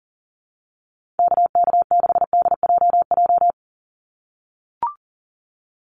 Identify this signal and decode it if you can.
Morse code, heard as a keyed tone, about 40 wpm, 700 Hz. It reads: KK6DJJ
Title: USA Call Signs 40wpm